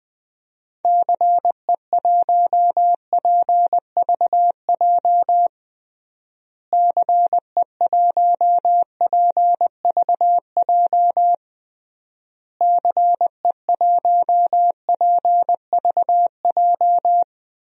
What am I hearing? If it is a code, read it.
Morse code, heard as a keyed tone, about 21 wpm, 700 Hz. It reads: CE1PVJ CE1PVJ CE1PVJ
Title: DXCC-100 Call Signs - 6 Characters - Repeated 3x - 20wpm